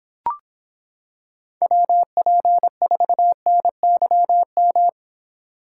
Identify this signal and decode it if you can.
Morse code, heard as a keyed tone, about 25 wpm, 700 Hz. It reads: WP4NYM